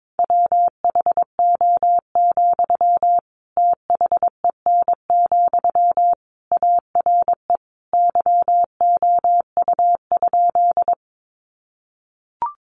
Morse code, heard as a keyed tone, about 22 wpm, 700 Hz. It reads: WHO, THEN, ARE YOU?
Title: Sentences from Top 100 Words 22wpm